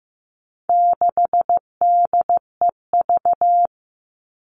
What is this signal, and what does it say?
Morse code, heard as a keyed tone, about 15 wpm, 700 Hz. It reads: 6DEV
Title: All Cut Numbers - 4-Digit 15wpm